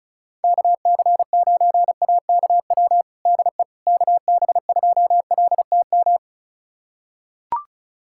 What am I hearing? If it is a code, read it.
Morse code, heard as a keyed tone, about 36 wpm, 700 Hz. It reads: KC9AKW DE KB2LTM
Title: USA Callsign DE Callsign 35wpm